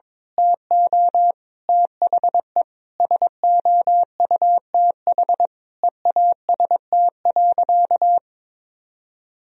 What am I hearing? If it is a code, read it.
Morse code, heard as a keyed tone, about 22 wpm, 700 Hz. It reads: TO THE SOUTH EAST.
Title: Sentences from Top 1000 Words - Review Part 3 - 22wpm